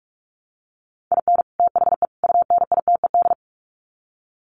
Morse code, heard as a keyed tone, about 44 wpm, 700 Hz. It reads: IN THE UNITED
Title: Sets of 3 Words 45wpm